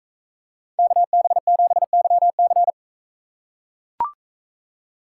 Morse code, heard as a keyed tone, about 40 wpm, 700 Hz. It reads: KB7YC